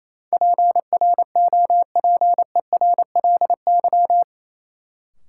Morse code, heard as a keyed tone, about 28 wpm, 700 Hz. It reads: PROPERLY
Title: Sentences from Top 1100 Words - New Word Drill Alt - 28wpm